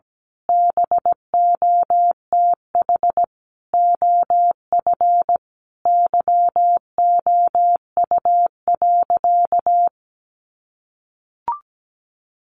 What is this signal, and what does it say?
Morse code, heard as a keyed tone, about 17 wpm, 700 Hz. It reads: BOTH OF YOU.
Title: Sentences from Top 1100 Words 17wpm